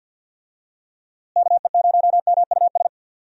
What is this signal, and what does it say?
Morse code, heard as a keyed tone, about 51 wpm, 700 Hz. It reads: KE0MUS
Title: USA Call Signs - 6 Characters - Repeated 3x - 50wpm